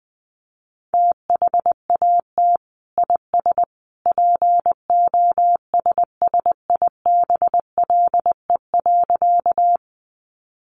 Morse code, heard as a keyed tone, about 20 wpm, 700 Hz. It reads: THAT IS POSSIBLE.
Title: Sentences from Top 500 Words 20wpm